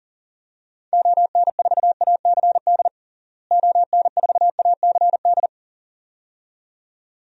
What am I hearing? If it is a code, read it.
Morse code, heard as a keyed tone, about 37 wpm, 700 Hz. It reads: ON4ACD ON4ACD
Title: DXCC-100 Call Signs - 6 Characters - Repeated 2x - 40wpm